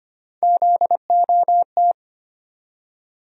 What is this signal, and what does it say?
Morse code, heard as a keyed tone, about 25 wpm, 700 Hz. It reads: ZOT
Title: USA Call Sign Postfixes - 3 Letters 25wpm